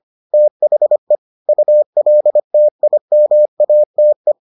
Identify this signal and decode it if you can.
Morse code, heard as a keyed tone, about 25 wpm, 600 Hz. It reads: THE ULTIMATE